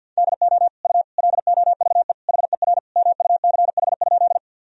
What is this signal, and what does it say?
Morse code, heard as a keyed tone, about 51 wpm, 700 Hz. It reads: DO U LOVE HER MUCH?
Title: Sentences from Top 400 Words and Abbreviations 50wpm